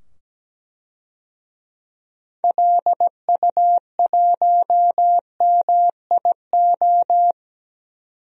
Morse code, heard as a keyed tone, about 17 wpm, 700 Hz. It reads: LU1MIO
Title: DXCC-100 Call Signs - 6 Characters - 17wpm